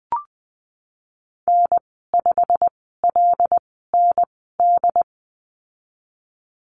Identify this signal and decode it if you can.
Morse code, heard as a keyed tone, about 20 wpm, spaced as for 15 wpm, 700 Hz. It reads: N5LND
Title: 147 Lesson 24g - Letter y - Call Signs 20wpm